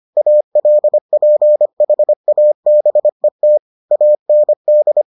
Morse code, heard as a keyed tone, about 25 wpm, 600 Hz. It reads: ALPHABET AND